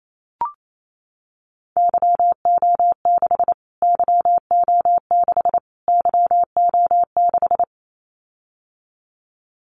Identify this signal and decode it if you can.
Morse code, heard as a keyed tone, about 28 wpm, 700 Hz. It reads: YO6 YO6 YO6
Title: Callsign Prefixes for Top 25 DXCC - Repeated 3x - 28wpm